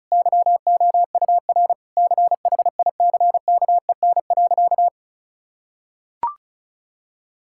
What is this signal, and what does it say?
Morse code, heard as a keyed tone, about 36 wpm, 700 Hz. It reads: YOUR CHICKEN.
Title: Sentences from Top 1000 Words - Review Part 5 - 35wpm